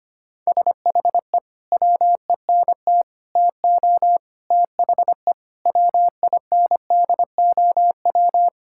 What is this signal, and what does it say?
Morse code, heard as a keyed tone, about 25 wpm, 700 Hz. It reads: SHE WENT TO THE WINDOW
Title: Sentences from Top 1600 Words 25wpm